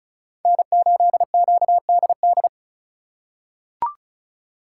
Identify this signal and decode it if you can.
Morse code, heard as a keyed tone, about 34 wpm, 700 Hz. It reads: N8QDD
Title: USA Call Signs - 5 Characters - 3x Character Spacing - 35wpm